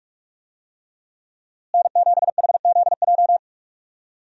Morse code, heard as a keyed tone, about 46 wpm, 700 Hz. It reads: N7HZJ
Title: USA Call Signs - 5 Characters - Repeated 3x - 45wpm